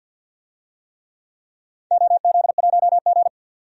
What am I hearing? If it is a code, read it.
Morse code, heard as a keyed tone, about 47 wpm, 700 Hz. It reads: OZ1G
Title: DXCC-100 Call Signs - 4 Characters - 50wpm